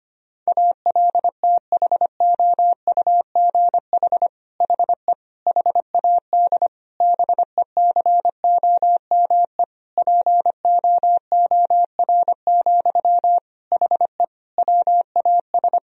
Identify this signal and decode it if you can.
Morse code, heard as a keyed tone, about 25 wpm, 700 Hz. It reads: ALTHOUGH HE HAD BECOME POOR, HE WAS
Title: Sentences from Top 1000 Words - Review Part 3 - 25wpm